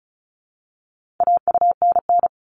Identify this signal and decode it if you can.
Morse code, heard as a keyed tone, about 34 wpm, 700 Hz. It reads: AUNN
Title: All Cut Numbers - 4-Digit 35wpm